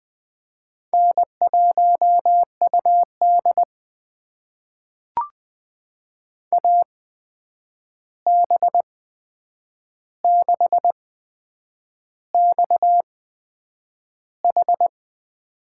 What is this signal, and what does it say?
Morse code, heard as a keyed tone, about 20 wpm, spaced as for 8 wpm, 700 Hz. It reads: N1UD AB6XH
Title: USA Call Signs - 8x Character Spacing - 20wpm